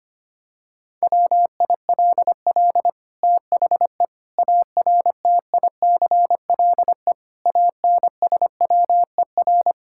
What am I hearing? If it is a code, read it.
Morse code, heard as a keyed tone, about 25 wpm, 700 Hz. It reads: WILL THE ARTICLE ANSWER
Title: Sentences from Top 1000 Words - Review Part 5 - 25wpm